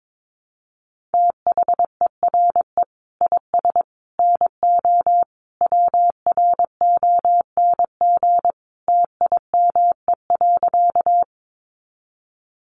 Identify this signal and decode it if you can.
Morse code, heard as a keyed tone, about 22 wpm, 700 Hz. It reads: THERE IS NO WRONG TIME.
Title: Sentences from Top 900 Words 22wpm